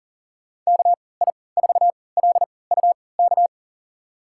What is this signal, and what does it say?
Morse code, heard as a keyed tone, about 39 wpm, spaced as for 22 wpm, 700 Hz. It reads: KI4LUK
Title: USA Call Signs - 3x Character Spacing - 40wpm